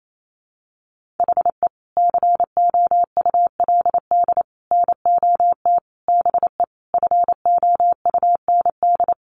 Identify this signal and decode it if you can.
Morse code, heard as a keyed tone, about 28 wpm, 700 Hz. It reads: HE COULD NOT BE FOUND
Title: Sentences from Top 1400 Words 28wpm